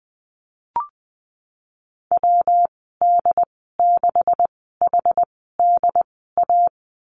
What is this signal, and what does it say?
Morse code, heard as a keyed tone, about 19 wpm, spaced as for 14 wpm, 700 Hz. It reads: WD6HDA
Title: 161 Lesson 26g - Letter g - Call Signs 20wpm